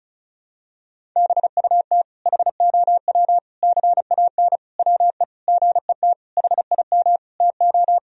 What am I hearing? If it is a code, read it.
Morse code, heard as a keyed tone, about 34 wpm, 700 Hz. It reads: BUT HOW CAN WE GET HIM TO
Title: Sentences from Top 1000 Words - Review Part 4 - 35wpm